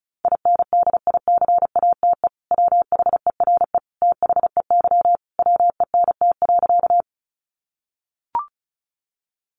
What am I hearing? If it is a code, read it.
Morse code, heard as a keyed tone, about 34 wpm, 700 Hz. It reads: INDICATE WHERE THEY WENT.
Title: Sentences from Top 1300 Words 35wpm